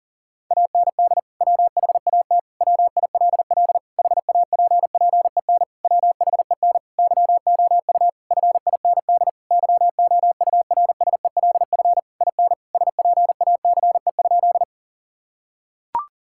Morse code, heard as a keyed tone, about 40 wpm, 700 Hz. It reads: AND WHAT WILL HAPPEN WHEN YOU FIND YOURSELF IN SPACE?
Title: Sentences from Top 1000 Words - Review Part 5 - 40wpm